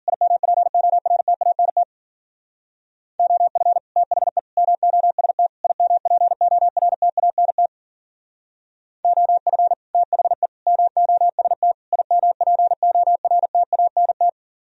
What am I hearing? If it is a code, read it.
Morse code, heard as a keyed tone, about 48 wpm, 700 Hz. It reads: IMPORTANT OF THE MOST IMPORTANT OF THE MOST IMPORTANT
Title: Speed-Racing - Sets of 4 Words 40wpm